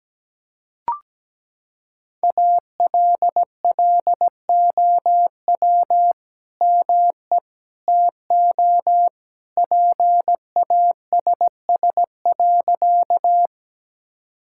Morse code, heard as a keyed tone, about 17 wpm, 700 Hz. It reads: ALLOW ME TO PASS.
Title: Sentences from Top 1000 Words - Review Part 5 - 17wpm